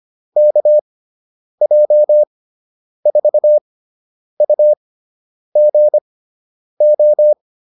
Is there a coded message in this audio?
Yes: Morse code from 0.4 to 7.3 s, heard as a keyed tone, about 25 wpm, spaced as for 9 wpm, 600 Hz. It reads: KJ4UGO